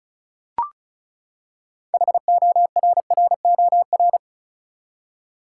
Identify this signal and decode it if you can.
Morse code, heard as a keyed tone, about 36 wpm, 700 Hz. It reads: HORROR
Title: Six Letter Words - 35wpm